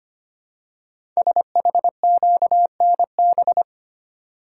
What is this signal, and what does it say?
Morse code, heard as a keyed tone, about 25 wpm, 700 Hz. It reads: SHQNB
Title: Code Groups - 5 Letters - 25wpm